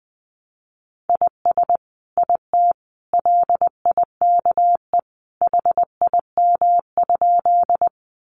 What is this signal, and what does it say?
Morse code, heard as a keyed tone, about 19 wpm, 700 Hz. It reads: IS IT LIKE HIM?
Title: Sentences from Top 100 Words 20wpm